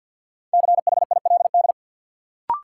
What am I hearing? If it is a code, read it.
Morse code, heard as a keyed tone, about 53 wpm, 700 Hz. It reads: CHILD
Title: Sentences from Top 800 Words - New Word Drill - 50wpm